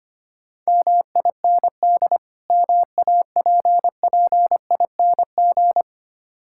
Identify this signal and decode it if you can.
Morse code, heard as a keyed tone, about 26 wpm, 700 Hz. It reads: MIND MAPPING
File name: practice / 25wpm / Difficult Words - Speed-Racing with Hints - 25wpm